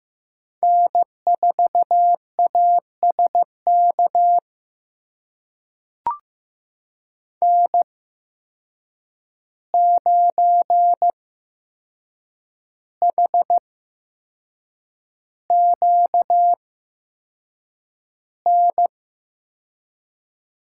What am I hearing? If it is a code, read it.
Morse code, heard as a keyed tone, about 15 wpm, spaced as for 6 wpm, 700 Hz. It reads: N4ASK N9HQN
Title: USA Call Signs - 5 Characters - 8x Character Spacing - 15wpm